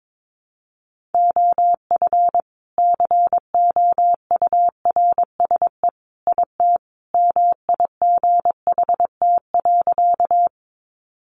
Morse code, heard as a keyed tone, about 22 wpm, 700 Hz. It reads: OF COURSE IT MIGHT.
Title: Sentences from Top 500 Words 22wpm